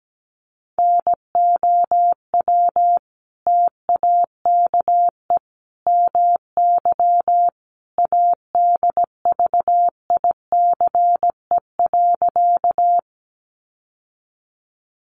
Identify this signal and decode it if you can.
Morse code, heard as a keyed tone, about 17 wpm, 700 Hz. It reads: NOW TAKE MY ADVICE.
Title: Sentences from Top 1100 Words 17wpm